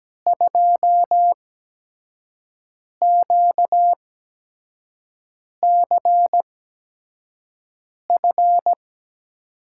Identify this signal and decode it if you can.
Morse code, heard as a keyed tone, about 17 wpm, spaced as for 5 wpm, 700 Hz. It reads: 2QCF
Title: USA Call Signs - 6 Characters - 8x Character Spacing - 17wpm